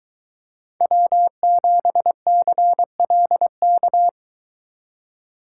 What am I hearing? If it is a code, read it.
Morse code, heard as a keyed tone, about 23 wpm, 700 Hz. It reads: W7CLK